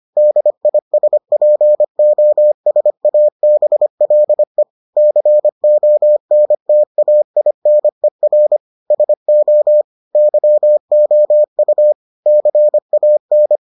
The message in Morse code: DISPOSABLE CONTAINER SO YOU CAN

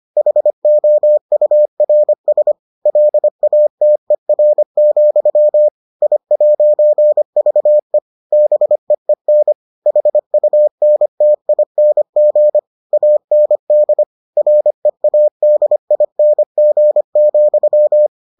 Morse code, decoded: HOURS LATER, I'VE BEEN HUNTING AND READING,

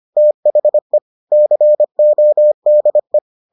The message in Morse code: THE CODE

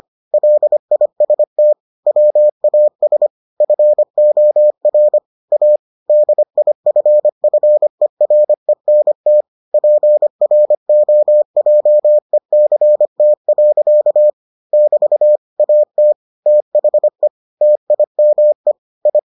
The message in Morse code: LIST WAS FOR A DIFFERENT PROJECT. <BT> AT THE TIME I

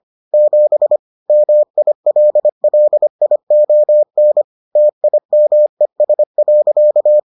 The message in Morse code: 7 MILLION TIMES.

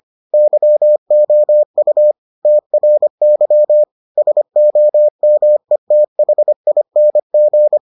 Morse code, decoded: YOU TRY SOMETHING

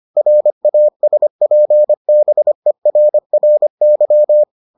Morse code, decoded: RASPBERRY

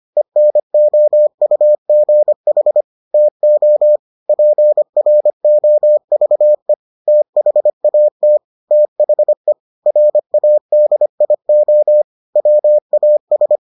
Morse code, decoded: ENOUGH TO PROVE THAT THE RADIO WAS

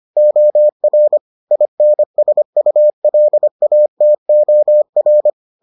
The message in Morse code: OR INSULATOR

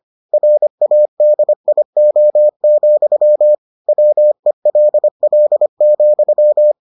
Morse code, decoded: RADIO, WELL,

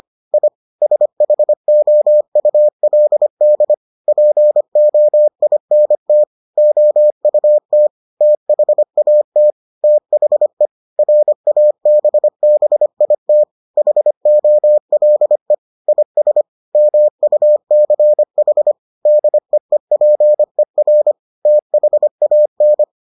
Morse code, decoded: I SHOULD POINT OUT THAT THE RABBIT HOLE IS MUCH DEEPER THAN

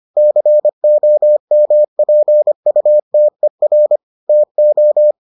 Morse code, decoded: COMPUTER TO

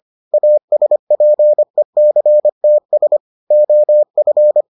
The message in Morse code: ASPECTS OF

